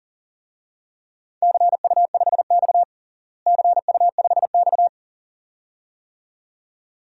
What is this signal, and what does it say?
Morse code, heard as a keyed tone, about 40 wpm, 700 Hz. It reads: CU5X CU5X
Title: DXCC-100 Call Signs - 4 Characters - Repeated 2x - 40wpm